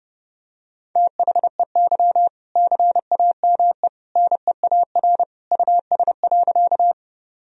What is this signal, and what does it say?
Morse code, heard as a keyed tone, about 30 wpm, 700 Hz. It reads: THEY CAME NEAR US.
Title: Sentences from Top 1000 Words - Review Part 1 - 30wpm